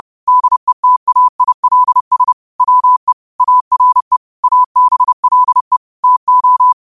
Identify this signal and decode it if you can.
Morse code, heard as a keyed tone, about 30 wpm, 1000 Hz. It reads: DETAILS WE ARE ABLE TO
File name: qotdfiles / 30wpm